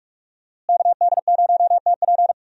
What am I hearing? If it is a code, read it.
Morse code, heard as a keyed tone, about 46 wpm, 700 Hz. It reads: KD0TP
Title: USA Call Signs - 5 Characters - Repeated 3x - 45wpm